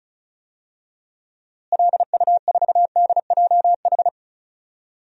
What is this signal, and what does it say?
Morse code, heard as a keyed tone, about 36 wpm, 700 Hz. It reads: LU4DJH
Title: DXCC-100 Call Signs - 6 Characters - Repeated 2x - 35wpm